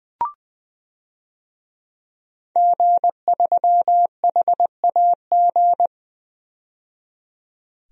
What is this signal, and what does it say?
Morse code, heard as a keyed tone, about 20 wpm, 700 Hz. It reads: G3HAG
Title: DXCC-100 Call Signs - 5 Characters - 20wpm